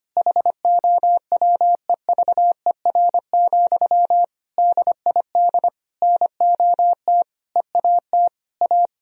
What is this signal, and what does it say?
Morse code, heard as a keyed tone, about 25 wpm, 700 Hz. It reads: HOWEVER, DID NOT EAT A
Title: Sentences from Top 1000 Words - Review Part 3 - 25wpm